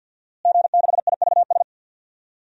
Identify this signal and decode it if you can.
Morse code, heard as a keyed tone, about 46 wpm, 700 Hz. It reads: G6IVS
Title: DXCC-100 Call Signs - 5 Characters - Repeated 3x - 50wpm